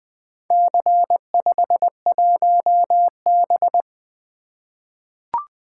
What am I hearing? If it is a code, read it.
Morse code, heard as a keyed tone, about 21 wpm, 700 Hz. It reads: C51B
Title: Four Letter-Number Combinations - 20wpm